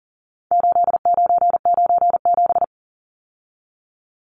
Morse code, heard as a keyed tone, about 40 wpm, 700 Hz. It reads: 8997
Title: Numbers - 4-Digit 40wpm